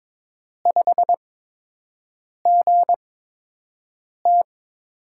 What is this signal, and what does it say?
Morse code, heard as a keyed tone, about 22 wpm, spaced as for 6 wpm, 700 Hz. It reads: 5GT
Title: USA Call Signs - 8x Character Spacing - 22wpm